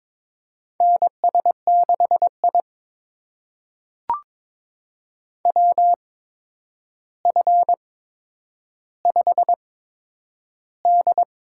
Morse code, heard as a keyed tone, about 21 wpm, spaced as for 9 wpm, 700 Hz. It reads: NS6I WF5D
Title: USA Call Signs - 4 Characters - 8x Character Spacing - 22wpm